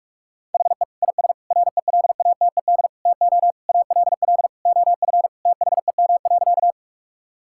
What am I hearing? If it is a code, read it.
Morse code, heard as a keyed tone, about 46 wpm, 700 Hz. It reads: HE IS RELATED TO ALL OF THEM.
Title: Sentences from Top 1000 Words - Review Part 5 - 45wpm